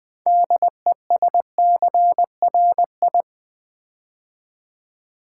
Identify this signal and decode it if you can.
Morse code, heard as a keyed tone, about 20 wpm, 700 Hz. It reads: DESCRI
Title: Build A Word - 2 Letters At A Time - 8-Letter Words - 20wpm